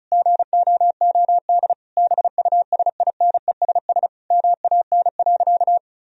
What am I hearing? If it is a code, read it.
Morse code, heard as a keyed tone, about 34 wpm, 700 Hz. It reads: GOOD BUSINESS MAN.
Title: Sentences from Top 1000 Words - Review Part 4 - 35wpm